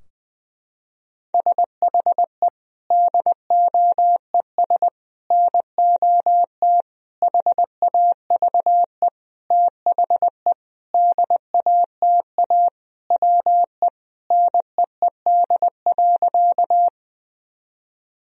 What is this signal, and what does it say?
Morse code, heard as a keyed tone, about 20 wpm, 700 Hz. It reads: SHE DOES NOT HAVE THE DATA WE NEED.
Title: Sentences from Top 1000 Words - Review Part 3 - 20wpm